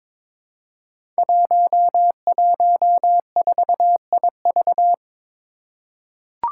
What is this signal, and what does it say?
Morse code, heard as a keyed tone, about 22 wpm, 700 Hz. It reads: 114IV
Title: Five Challenging Letter-Number Combinations 22wpm